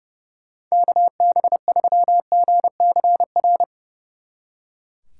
Morse code, heard as a keyed tone, about 38 wpm, 700 Hz. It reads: KB3GCR
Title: Speed Racing - USA 2x3 Call Signs 30wpm